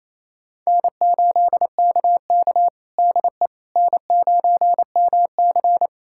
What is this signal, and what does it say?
Morse code, heard as a keyed tone, about 28 wpm, 700 Hz. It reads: N8KK DE N9MC
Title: USA Callsign DE Callsign - 4 Characters - 28wpm